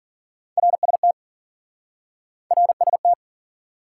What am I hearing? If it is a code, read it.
Morse code, heard as a keyed tone, about 47 wpm, 700 Hz. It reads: RST RST
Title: Speed-Racing Top 90 QSO Elements 40wpm